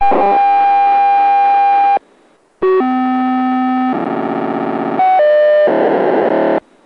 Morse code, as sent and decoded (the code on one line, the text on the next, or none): none